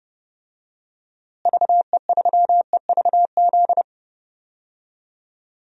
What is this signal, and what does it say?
Morse code, heard as a keyed tone, about 30 wpm, 700 Hz. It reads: VE3EVZ
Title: DXCC-100 Call Signs - 6 Characters - Repeated 3x - 30wpm